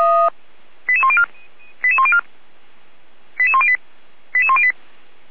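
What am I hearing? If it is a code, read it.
Five-tone selective call (ZVEI), heard as quick runs of five tones, tones of about 70 ms. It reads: 88104 88104 88108 88108
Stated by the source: ZVEI five-tone call received by SDR, from Feuerwehr Orlamünde's audio server.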